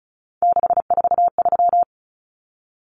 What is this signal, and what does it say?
Morse code, heard as a keyed tone, about 34 wpm, 700 Hz. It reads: 643
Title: Numbers - 3-Digit 35wpm